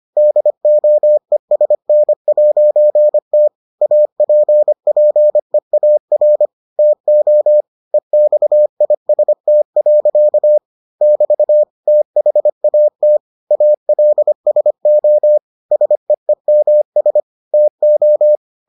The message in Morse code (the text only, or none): DOESN'T APPEAR TO EXIST. <BT> THAT ALSO SEEMS TO